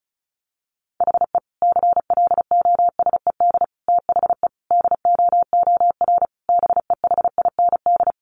Morse code, heard as a keyed tone, about 34 wpm, 700 Hz. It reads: HE CLOSED THE DOOR BEHIND
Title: Sentences from Top 1400 Words 35wpm